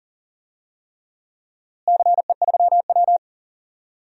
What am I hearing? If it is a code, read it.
Morse code, heard as a keyed tone, about 40 wpm, 700 Hz. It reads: CE3W
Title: DXCC-100 Call Signs - 4 Characters - Repeated 3x - 40wpm